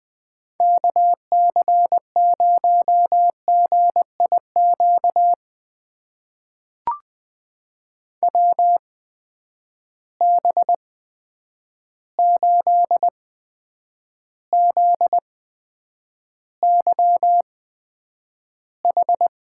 Morse code, heard as a keyed tone, about 20 wpm, spaced as for 9 wpm, 700 Hz. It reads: KC0GIQ WB8ZYH
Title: USA Call Signs - 8x Character Spacing - 20wpm